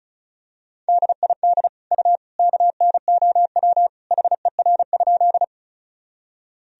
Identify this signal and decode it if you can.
Morse code, heard as a keyed tone, about 36 wpm, 700 Hz. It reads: DID U KNOW HER?
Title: Sentences from Top 100 Words and Abbreviations 35wpm